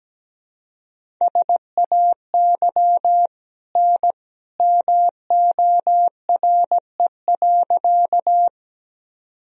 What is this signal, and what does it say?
Morse code, heard as a keyed tone, about 17 wpm, 700 Hz. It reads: SAY N MORE.
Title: Sentences from Top 200 Words and Abbreviations 17wpm